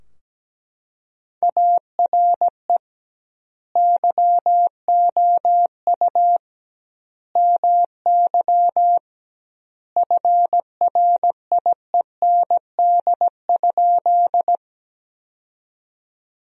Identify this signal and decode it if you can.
Morse code, heard as a keyed tone, about 17 wpm, 700 Hz. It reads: ARE YOU MY FRIEND?
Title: 2x Word-Spacing - Sentences from Top 300 Words - 17wpm